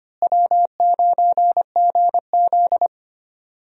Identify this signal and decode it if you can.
Morse code, heard as a keyed tone, about 25 wpm, 700 Hz. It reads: W9GZ
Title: USA Call Signs - 4 Characters - Repeated 3x - 25wpm